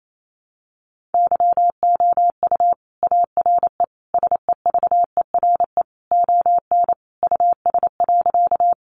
Morse code, heard as a keyed tone, about 28 wpm, 700 Hz. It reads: YOU ARE SEVERE ON US.
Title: Sentences from Top 1400 Words 28wpm